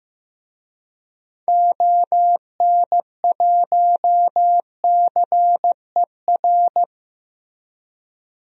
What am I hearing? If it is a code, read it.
Morse code, heard as a keyed tone, about 15 wpm, 700 Hz. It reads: ON1CER
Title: DXCC-100 Call Signs - 6 Characters - 15wpm